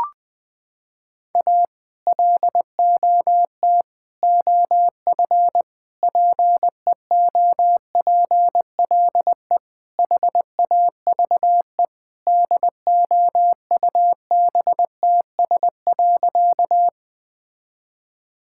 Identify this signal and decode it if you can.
Morse code, heard as a keyed tone, about 21 wpm, 700 Hz. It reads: A LOT OF PEOPLE HAVE DOUBTS.
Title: Sentences with 5-Word Phrases - 20wpm